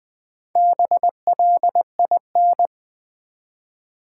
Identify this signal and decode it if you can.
Morse code, heard as a keyed tone, about 20 wpm, 700 Hz. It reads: BLIN
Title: Build A Word - 1 Letter At A Time - 3 to 5-Letter Words - 20wpm